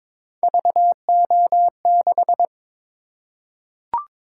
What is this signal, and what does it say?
Morse code, heard as a keyed tone, about 22 wpm, 700 Hz. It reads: VO6
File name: practice / 22wpm / Three Letter-Number Combinations - 22wpm